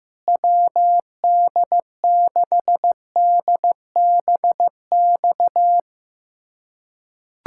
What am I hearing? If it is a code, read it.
Morse code, heard as a keyed tone, about 16 wpm, 700 Hz. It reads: WD6DBX
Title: USA Callsign DE Callsign - 6 Characters - 15wpm